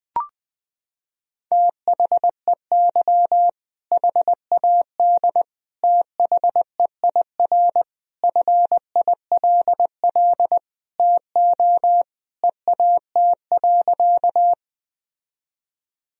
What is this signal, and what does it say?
Morse code, heard as a keyed tone, about 21 wpm, 700 Hz. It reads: THEY HAD THEIR FILL TO EAT.
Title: Sentences from Top 1000 Words - Review Part 3 - 20wpm